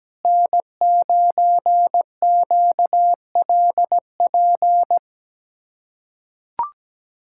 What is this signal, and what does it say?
Morse code, heard as a keyed tone, about 17 wpm, 700 Hz. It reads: N9QLP
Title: USA Callsign DE Callsign - 5 Characters - 17wpm